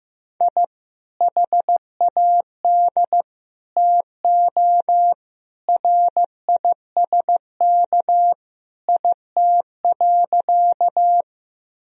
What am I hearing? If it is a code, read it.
Morse code, heard as a keyed tone, about 16 wpm, 700 Hz. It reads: I HAD TO RISK IT.
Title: Sentences from Top 1000 Words - Review Part 4 - 15wpm